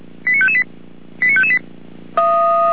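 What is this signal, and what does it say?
Five-tone selective call (ZVEI), heard as quick runs of five tones, tones of about 70 ms. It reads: 89558 89558
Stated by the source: Five-tone ZVEI alert received by SDR for the Feuerwehr Orlamünde fire brigade.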